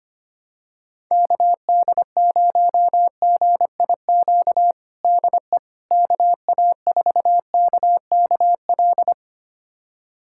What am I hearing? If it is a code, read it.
Morse code, heard as a keyed tone, about 25 wpm, 700 Hz. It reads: KD0GIQ DE KA4KKL
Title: USA Callsign DE Callsign - 6 Characters - 25wpm